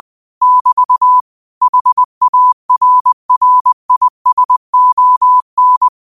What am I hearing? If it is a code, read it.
Morse code, heard as a keyed tone, about 20 wpm, 1000 Hz. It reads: <BT> HARRISON